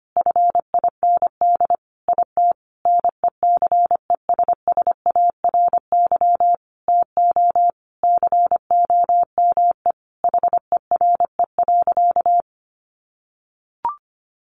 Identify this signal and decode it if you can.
Morse code, heard as a keyed tone, about 25 wpm, 700 Hz. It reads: FIND IT NECESSARY TO COME HERE.
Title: Sentences from Top 1300 Words 25wpm